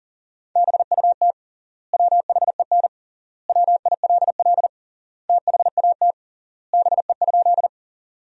Morse code, heard as a keyed tone, about 37 wpm, 700 Hz. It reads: BUT WHEN WILL THAT BE?
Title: 3x Word-Spacing - Sentences from Top 100 Words - 40wpm